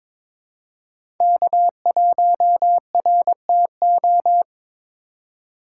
Morse code, heard as a keyed tone, about 21 wpm, 700 Hz. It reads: K1RTO
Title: USA Call Signs - 5 Characters - 6x Character Spacing - 22wpm